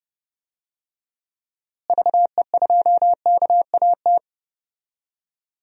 Morse code, heard as a keyed tone, about 30 wpm, 700 Hz. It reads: VE2KAT
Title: DXCC-100 Call Signs - 6 Characters - Repeated 2x - 30wpm